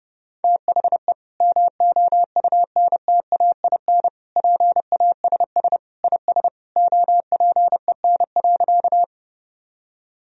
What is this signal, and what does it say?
Morse code, heard as a keyed tone, about 30 wpm, 700 Hz. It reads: THE MOUNTAIN PASS IS OPEN.
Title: Sentences from Top 1000 Words - Review Part 1 - 30wpm